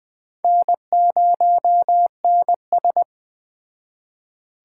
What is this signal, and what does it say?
Morse code, heard as a keyed tone, about 21 wpm, 700 Hz. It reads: N0NS
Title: Speed Racing - USA 1x2 Call Signs 17wpm